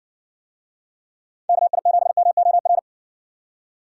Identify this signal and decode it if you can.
Morse code, heard as a keyed tone, about 58 wpm, 700 Hz. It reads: KI7MOR